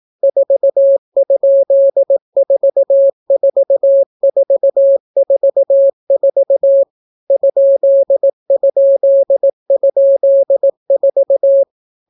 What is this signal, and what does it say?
Morse code, heard as a keyed tone, about 18 wpm, 550 Hz. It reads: 4?44444 ???4